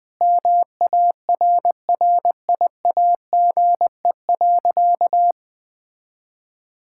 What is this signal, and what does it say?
Morse code, heard as a keyed tone, about 20 wpm, 700 Hz. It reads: MARRIAGE.
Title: Sentences from Top 1000 Words - Review Part 6 - 20wpm